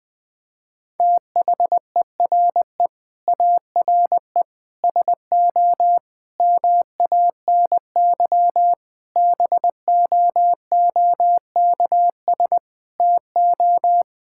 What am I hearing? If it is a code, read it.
Morse code, heard as a keyed tone, about 20 wpm, 700 Hz. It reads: THERE ARE SO MANY BOOKS TO
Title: Sentences with 4-Word Phrases - 20wpm